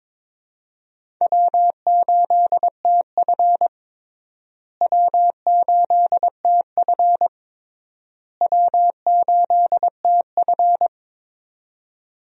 Morse code, heard as a keyed tone, about 22 wpm, 700 Hz. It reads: W8TF W8TF W8TF
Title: USA Call Signs - 4 Characters - Repeated 3x - 22wpm